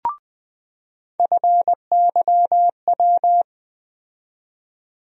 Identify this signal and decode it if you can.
Morse code, heard as a keyed tone, about 20 wpm, 700 Hz. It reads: FYW
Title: USA Call Sign Postfixes - 3 Letters 20wpm